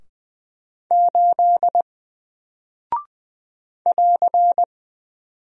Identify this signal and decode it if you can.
Morse code, heard as a keyed tone, about 20 wpm, 700 Hz. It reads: 8 <AR>
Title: Single Character (Letter:Number:Punctuation:Prosign) Alt 20wpm